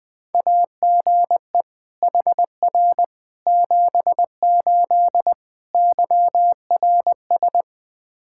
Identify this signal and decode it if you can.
Morse code, heard as a keyed tone, about 19 wpm, 700 Hz. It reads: AGE HR 78 YRS
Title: Practice QSO - Exchange 3 - More Info - 20wpm